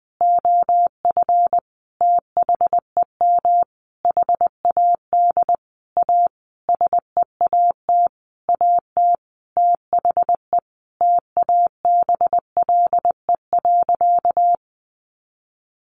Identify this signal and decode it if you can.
Morse code, heard as a keyed tone, about 20 wpm, 700 Hz. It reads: OF THEM HAD A SEAT AT THE TABLE.
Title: Sentences from Top 1500 Words 20wpm